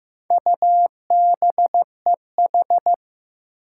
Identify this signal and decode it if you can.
Morse code, heard as a keyed tone, about 15 wpm, 700 Hz. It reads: UBEH
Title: Four Challenging Letter-Number Combinations 15wpm